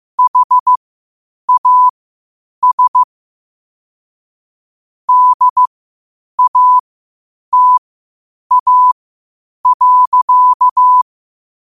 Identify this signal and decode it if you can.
Morse code, heard as a keyed tone, about 15 wpm, spaced as for 8 wpm, 1000 Hz. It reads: HAS DATA.
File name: qotdfiles / 8wpm